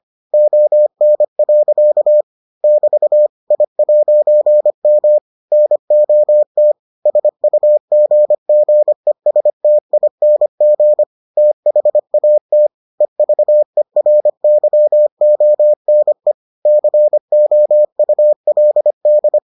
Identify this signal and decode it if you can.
Morse code, heard as a keyed tone, about 25 wpm, 600 Hz. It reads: ON. <BT> I'M NOT SUGGESTING THAT EVERYONE COULD